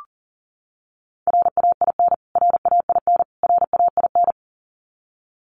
Morse code, heard as a keyed tone, about 40 wpm, 700 Hz. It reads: RAIN RAIN RAIN
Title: Top 500 Words Repeated 40wpm